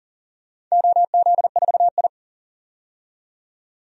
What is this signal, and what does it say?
Morse code, heard as a keyed tone, about 40 wpm, 700 Hz. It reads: OZ4I
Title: DXCC-100 Call Signs - 4 Characters - 40wpm